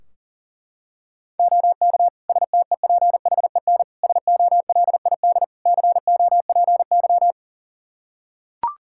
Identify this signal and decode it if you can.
Morse code, heard as a keyed tone, about 40 wpm, 700 Hz. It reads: OK STEPHEN SOLID COPY
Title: K1USN SST QSO Practice 40wpm